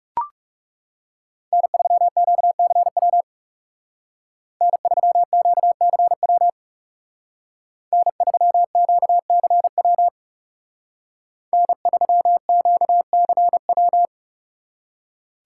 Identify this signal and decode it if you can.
Morse code, heard as a keyed tone, about 35 wpm, 700 Hz. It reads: N3QCW N3QCW N3QCW N3QCW
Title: Speed Racing - USA 1x3 Call Signs 30wpm